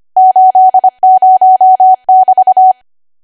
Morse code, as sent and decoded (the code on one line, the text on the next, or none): ---.. ----- -...-
80<BT>